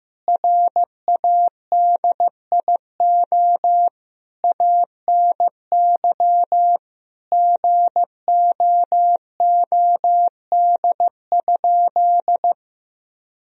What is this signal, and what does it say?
Morse code, heard as a keyed tone, about 14 wpm, 700 Hz. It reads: RADIO ANY GOOD?
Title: Sentences from Top 1000 Words - Review Part 3 - 15wpm